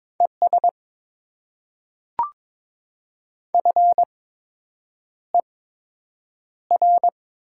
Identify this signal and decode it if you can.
Morse code, heard as a keyed tone, about 22 wpm, spaced as for 8 wpm, 700 Hz. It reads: ES FER
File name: Top 20 QSO Elements - 8x Character Spacing - 22wpm